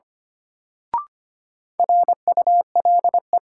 Morse code, heard as a keyed tone, about 25 wpm, 700 Hz. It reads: RULE
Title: Sentences from Top 400 Words - New Word Drill - 25wpm